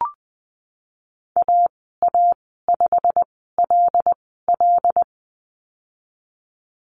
Morse code, heard as a keyed tone, about 20 wpm, spaced as for 15 wpm, 700 Hz. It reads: AA5LL
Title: 088 Lesson 15g - Number 5 - Call Signs 20wpm